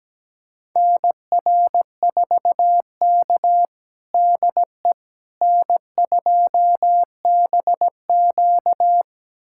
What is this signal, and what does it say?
Morse code, heard as a keyed tone, about 17 wpm, 700 Hz. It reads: NR4K DE N2BQ
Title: USA Callsign DE Callsign - 4 Characters - 17wpm